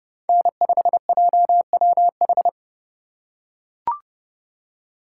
Morse code, heard as a keyed tone, about 30 wpm, 700 Hz. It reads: N5JWH
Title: USA Call Signs - 3x Character Spacing - 30wpm